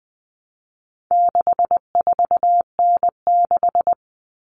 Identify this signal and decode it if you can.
Morse code, heard as a keyed tone, about 20 wpm, 700 Hz. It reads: 64N6
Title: All Cut Numbers - 4-Digit 20wpm